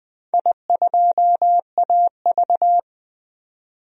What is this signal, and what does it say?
Morse code, heard as a keyed tone, about 20 wpm, 700 Hz. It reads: I2AV
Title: DXCC-100 Call Signs - 4 Characters - 20wpm